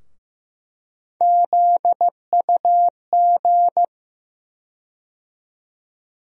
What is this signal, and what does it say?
Morse code, heard as a keyed tone, about 15 wpm, 700 Hz. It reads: ZUG
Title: USA Call Sign Postfixes - 3 Letters 15wpm